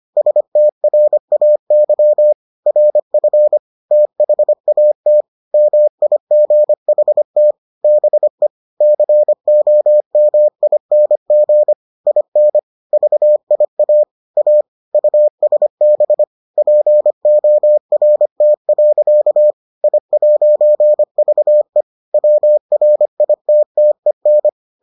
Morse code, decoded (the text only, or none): STRAY RF THAT MIGHT BE COMING IN VIA A USB PORT. I'VE WRITTEN